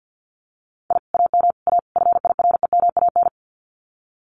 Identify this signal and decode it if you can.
Morse code, heard as a keyed tone, about 50 wpm, 700 Hz. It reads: I AM A FIREMAN
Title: Common QSO Phrases Encore 50wpm